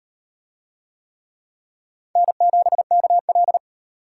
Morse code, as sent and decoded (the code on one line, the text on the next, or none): -. --... -.- .-..
N7KL